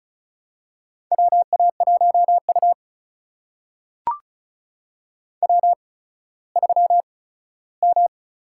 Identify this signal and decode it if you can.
Morse code, heard as a keyed tone, about 36 wpm, spaced as for 17 wpm, 700 Hz. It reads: WA1U W3M
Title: USA Call Signs - 8x Character Spacing - 35wpm